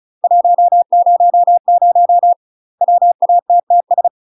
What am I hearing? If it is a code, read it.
Morse code, heard as a keyed tone, about 36 wpm, 700 Hz. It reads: 100 WATTS
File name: UK MASTER QSOs TRAINING / 35WPM / UK MASTER QSOs 0020